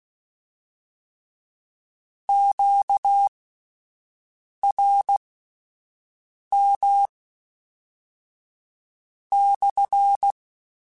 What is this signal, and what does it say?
Morse code, heard as a keyed tone, about 16 wpm, spaced as for 5 wpm, 800 Hz. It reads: QRM /
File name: Q Calls / Q Calls-mp3s / Q CALLS SPSL @10